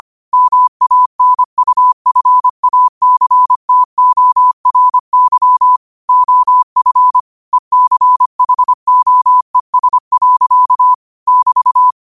Morse code, heard as a keyed tone, about 25 wpm, 1000 Hz. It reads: MANUFACTORY OF ECHOES. <BT>
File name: qotdfiles / 25wpm